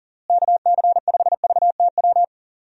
Morse code, heard as a keyed tone, about 40 wpm, 700 Hz. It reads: KC5VTW
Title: USA Call Signs - 6 Characters - Repeated 2x - 40wpm